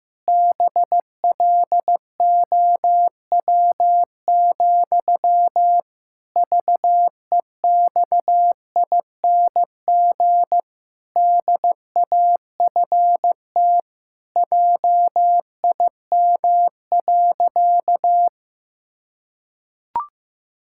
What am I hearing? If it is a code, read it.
Morse code, heard as a keyed tone, about 16 wpm, 700 Hz. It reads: BLOW, VEXING DAFT JIM.
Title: Pangram Challenge v2 - 15wpm